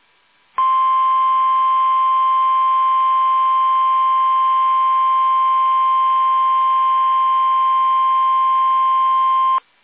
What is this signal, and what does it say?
Emergency Alert System (SAME) tones (the 1050 Hz attention tone). It is heard from 0.6 to 8.9 s.
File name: eas / KUSO / KUSO